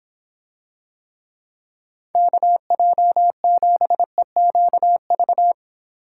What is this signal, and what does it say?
Morse code, heard as a keyed tone, about 25 wpm, 700 Hz. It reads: KJ7EQV